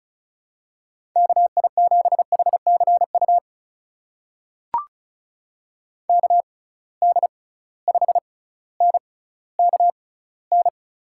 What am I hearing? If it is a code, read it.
Morse code, heard as a keyed tone, about 36 wpm, spaced as for 18 wpm, 700 Hz. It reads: KI7HCU KD5NKN
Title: USA Call Signs - 6 Characters - 6x Character Spacing - 35wpm